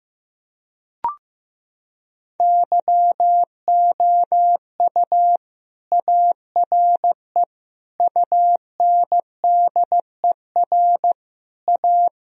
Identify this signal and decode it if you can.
Morse code, heard as a keyed tone, about 16 wpm, 700 Hz. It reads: YOU ARE UNDER A
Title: Sentences from Top 1000 Words - Review Part 1 - 15wpm